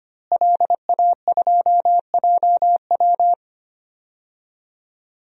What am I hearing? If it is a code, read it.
Morse code, heard as a keyed tone, about 25 wpm, 700 Hz. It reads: LA2JW
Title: DXCC-100 Call Signs - 5 Characters - Repeated 2x - 25wpm